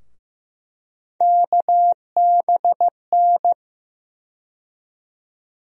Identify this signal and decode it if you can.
Morse code, heard as a keyed tone, about 15 wpm, 700 Hz. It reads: KBN
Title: USA Call Sign Postfixes - 3 Letters 15wpm